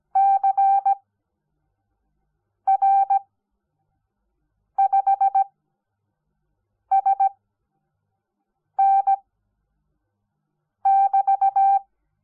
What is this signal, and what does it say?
Morse code, heard as a keyed tone, about 17 wpm, spaced as for 5 wpm, 780 Hz. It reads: CR5SN <BT>
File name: Onlinelehrgang / morsen / DJ4UF Morsen05